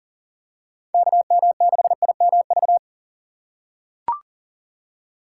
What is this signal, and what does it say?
Morse code, heard as a keyed tone, about 40 wpm, 700 Hz. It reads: KM6IMV
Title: USA Call Signs - 6 Characters - 4x Character Spacing - 40wpm